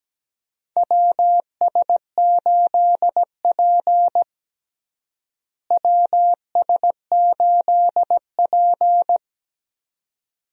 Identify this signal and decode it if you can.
Morse code, heard as a keyed tone, about 17 wpm, 700 Hz. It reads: WS8P WS8P
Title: USA Call Signs - 4 Characters - Repeated 2x - 17wpm